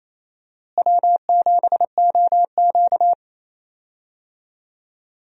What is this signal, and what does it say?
Morse code, heard as a keyed tone, about 27 wpm, 700 Hz. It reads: W7OQ
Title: USA Call Signs - 4 Characters - Repeated 3x - 28wpm